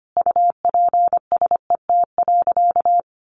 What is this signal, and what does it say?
Morse code, heard as a keyed tone, about 25 wpm, 700 Hz. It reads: UPSET.
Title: Sentences from Top 1200 Words 25wpm